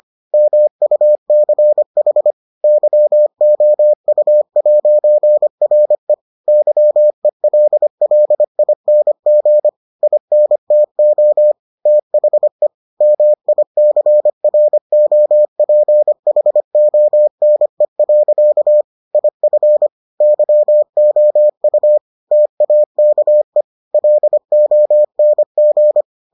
Morse code, heard as a keyed tone, about 25 wpm, 600 Hz. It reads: MUCH YOU'RE YELLING INTO THE MICROPHONE. IF YOU TAKE LONG